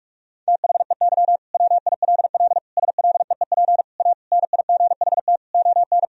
Morse code, heard as a keyed tone, about 41 wpm, 700 Hz. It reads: THEY WILL SLEEP A NIGHT ON
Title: Sentences from Top 1000 Words - Review Part 2 - 45wpm